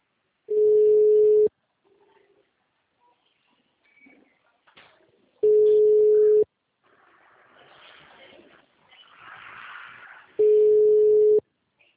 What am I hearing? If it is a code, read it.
Ringback tone (Europe, 425 Hz).
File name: out- -2016- - - .37287